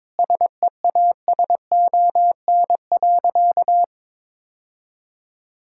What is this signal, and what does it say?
Morse code, heard as a keyed tone, about 22 wpm, 700 Hz. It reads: SEASON.
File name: Sentences with 7-Word Phrases - 22wpm